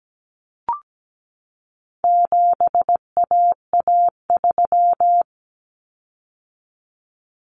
Morse code, heard as a keyed tone, about 17 wpm, 700 Hz. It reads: 7AA3
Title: Common Cut Numbers - 4-Digit 17wpm